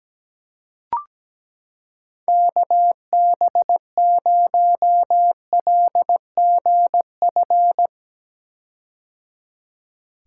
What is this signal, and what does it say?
Morse code, heard as a keyed tone, about 17 wpm, 700 Hz. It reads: KB0LGF
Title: K1USN SST QSO Practice - 17wpm